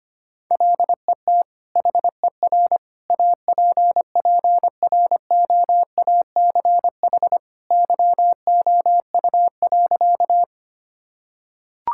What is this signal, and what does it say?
Morse code, heard as a keyed tone, about 25 wpm, 700 Hz. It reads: LET HER APPROACH YOU.
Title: Sentences from Top 1000 Words - Review Part 6 - 25wpm